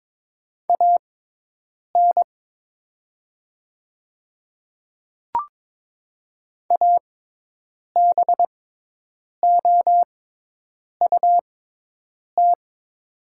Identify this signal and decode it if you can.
Morse code, heard as a keyed tone, about 21 wpm, spaced as for 8 wpm, 700 Hz. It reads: AN ABOUT
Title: 6x Character Spacing - Top 300 Words - 22wpm